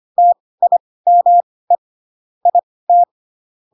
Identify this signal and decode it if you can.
Morse code, heard as a keyed tone, about 25 wpm, spaced as for 18 wpm, 700 Hz. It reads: TIME IT
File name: Pinocchio / Pinocchio ch1 12 25 ik1hge